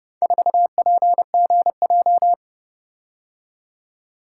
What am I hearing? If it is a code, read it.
Morse code, heard as a keyed tone, about 30 wpm, 700 Hz. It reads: 4PGJ